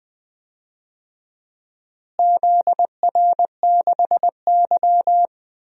Morse code, heard as a keyed tone, about 20 wpm, 700 Hz. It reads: ZR6Y
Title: DXCC-100 Call Signs - 4 Characters - Repeated 2x - 20wpm